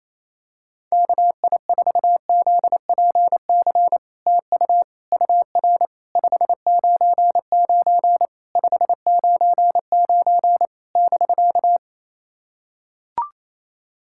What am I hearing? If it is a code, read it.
Morse code, heard as a keyed tone, about 28 wpm, 690 Hz. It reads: KI4ZPC TU UR 599 599 <BK>
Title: SOTA QSO Practice 28wpm